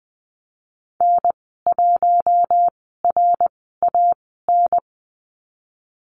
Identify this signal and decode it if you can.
Morse code, heard as a keyed tone, about 20 wpm, spaced as for 15 wpm, 700 Hz. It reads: N1RAN
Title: 076 Lesson 13g - Letter L - Call Signs 20wpm